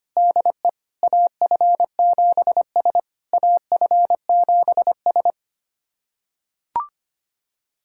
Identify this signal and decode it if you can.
Morse code, heard as a keyed tone, about 25 wpm, 690 Hz. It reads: DE AF7S AF7S
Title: SOTA QSO Practice 25wpm